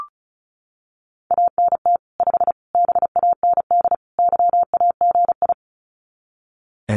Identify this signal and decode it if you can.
Morse code, heard as a keyed tone, about 34 wpm, 700 Hz. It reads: ANT 5 BAND YAGI
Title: Common QSO Phrases 35wpm